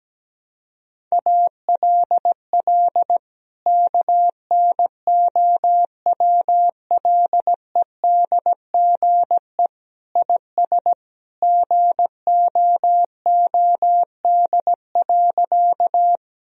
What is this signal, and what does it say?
Morse code, heard as a keyed tone, about 17 wpm, 700 Hz. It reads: ALL KNOWLEDGE IS GOOD.
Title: Sentences from Top 1000 Words - Review Part 3 - 17wpm